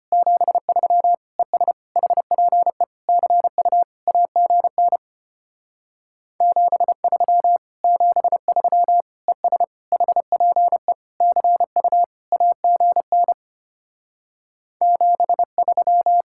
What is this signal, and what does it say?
Morse code, heard as a keyed tone, about 33 wpm, 700 Hz. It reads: 73 ES HPE CU AGN 73 73 ES HPE CU AGN 73